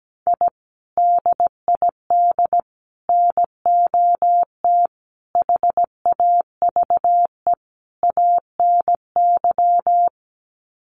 Morse code, heard as a keyed tone, about 17 wpm, 700 Hz. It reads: I DID NOT HAVE ANY
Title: Sets of 5 Words Encore 17wpm